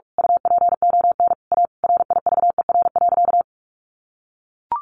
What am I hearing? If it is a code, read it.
Morse code, heard as a keyed tone, about 44 wpm, 700 Hz. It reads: UPON A RIVER.
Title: Sentences from Top 1100 Words 45wpm